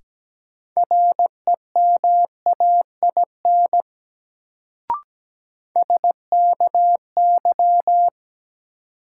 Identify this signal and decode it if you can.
Morse code, heard as a keyed tone, about 17 wpm, 700 Hz. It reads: REMAIN SKY
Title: Sentences from Top 1200 Words - New Word Drill - 17wpm